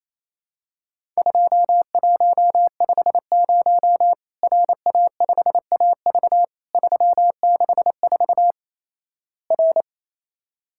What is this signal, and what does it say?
Morse code, heard as a keyed tone, about 27 wpm, 700 Hz. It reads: 2150 RA5AV 364 R
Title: Worked All Europe DX Contest - QTC Exchange - 28wpm